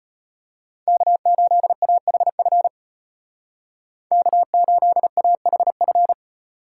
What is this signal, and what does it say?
Morse code, heard as a keyed tone, about 31 wpm, 700 Hz. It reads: K8AHF K8AHF